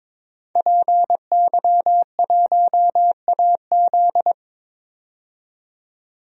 Morse code, heard as a keyed tone, about 21 wpm, 700 Hz. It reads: PY1AZ
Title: DXCC-100 Call Signs - 5 Characters - 22wpm